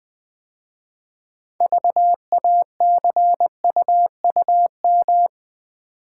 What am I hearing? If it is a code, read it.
Morse code, heard as a keyed tone, about 20 wpm, 700 Hz. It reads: VACUUM
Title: 6 Letter Words - First 3 Letters Given - 20wpm